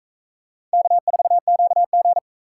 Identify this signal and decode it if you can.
Morse code, heard as a keyed tone, about 40 wpm, 700 Hz. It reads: K4QG